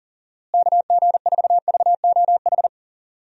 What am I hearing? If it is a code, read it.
Morse code, heard as a keyed tone, about 40 wpm, 700 Hz. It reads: KG4VOH
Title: USA Call Signs - 6 Characters - Repeated 3x - 40wpm